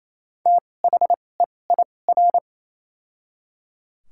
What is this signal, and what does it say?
Morse code, heard as a keyed tone, about 28 wpm, spaced as for 20 wpm, 700 Hz. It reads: THEIR
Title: 2x Character Spacing - Top 100 Words - 28wpm